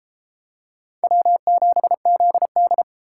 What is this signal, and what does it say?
Morse code, heard as a keyed tone, about 32 wpm, 700 Hz. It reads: W7ZD